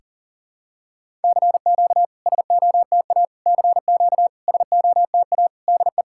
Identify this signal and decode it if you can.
Morse code, heard as a keyed tone, about 40 wpm, 700 Hz. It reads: CQ SOTA CQ SOTA DE
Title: SOTA QSO Practice 40wpm